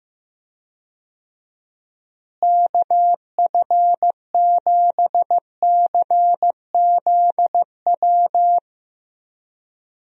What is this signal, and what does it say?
Morse code, heard as a keyed tone, about 15 wpm, 700 Hz. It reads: KF7CZW
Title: USA Call Signs - 6 Characters - Repeated 3x - 15wpm